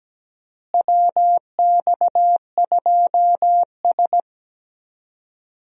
Morse code, heard as a keyed tone, about 17 wpm, 700 Hz. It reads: WX2S
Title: USA Call Signs - 4 Characters - Repeated 3x - 17wpm